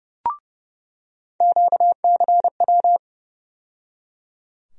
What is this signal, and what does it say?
Morse code, heard as a keyed tone, about 30 wpm, 700 Hz. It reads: QCW
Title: USA Call Sign Postfixes - 3 Letters 30wpm